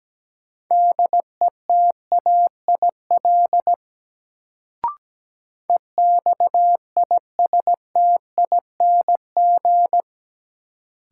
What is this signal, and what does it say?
Morse code, heard as a keyed tone, about 17 wpm, 700 Hz. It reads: DETAIL EXISTING
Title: Sentences from Top 1500 Words - New Word Drill Alt - 17wpm